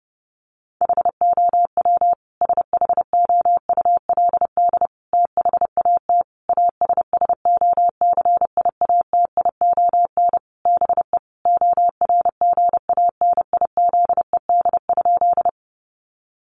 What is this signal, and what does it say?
Morse code, heard as a keyed tone, about 30 wpm, 700 Hz. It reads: HOW SHOULD THAT ASSOCIATION BE ORGANIZED?
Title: Sentences from Top 1300 Words 30wpm